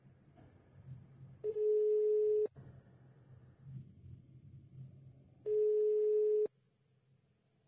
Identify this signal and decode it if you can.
Ringback tone (425 Hz, 1 s on, 3 s off).
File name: sauvegarde / ACRRecordings- -jusquau 17-juin-2017 / UBO Banque De La Post pour ASBL 2017 05 22 16 14 01